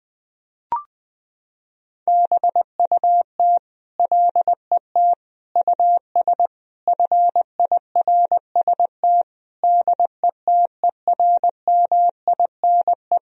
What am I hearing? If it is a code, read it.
Morse code, heard as a keyed tone, about 20 wpm, 700 Hz. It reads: BUT LET US FIRST DETERMINE
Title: Sentences from Top 1000 Words - Review Part 4 - 20wpm